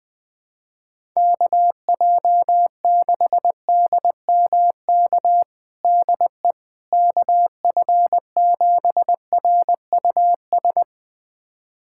Morse code, heard as a keyed tone, about 20 wpm, 700 Hz. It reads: KJ6DMK DE KF7RUS
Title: USA Callsign DE Callsign - 6 Characters - 20wpm